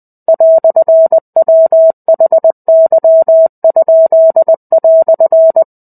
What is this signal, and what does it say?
Morse code, heard as a keyed tone, about 20 wpm, 650 Hz. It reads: "WHY?"
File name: 05-Chapter-05 20 WPM